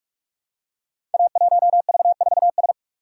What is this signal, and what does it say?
Morse code, heard as a keyed tone, about 44 wpm, 700 Hz. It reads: A1V4S
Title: Code Groups - 5 Letter:Numbers - 45wpm